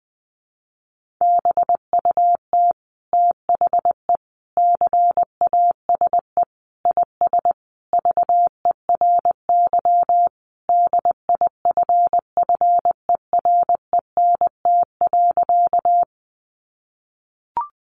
Morse code, heard as a keyed tone, about 20 wpm, 700 Hz. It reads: BUT THE CASE IS VERY DIFFERENT.
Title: Sentences from Top 700 Words 20wpm